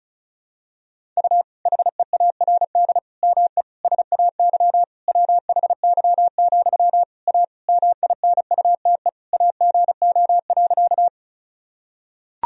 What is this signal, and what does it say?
Morse code, heard as a keyed tone, about 34 wpm, 700 Hz. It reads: U HEARD ME SAY WHY, A MINUTE AGO.
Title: Sentences from Top 500 Words and Abbreviations 35wpm